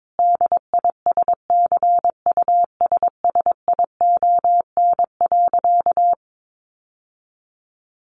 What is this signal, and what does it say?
Morse code, heard as a keyed tone, about 22 wpm, 700 Hz. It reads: DISCUSSION.
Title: Sentences from Top 1500 Words 22wpm